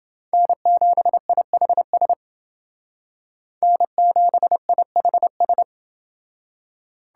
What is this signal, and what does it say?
Morse code, heard as a keyed tone, about 30 wpm, 700 Hz. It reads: N7IHS N7IHS